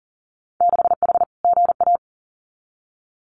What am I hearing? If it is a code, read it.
Morse code, heard as a keyed tone, about 39 wpm, 700 Hz. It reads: 6H GA
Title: Winter Field Day Exchanges Encore 40wpm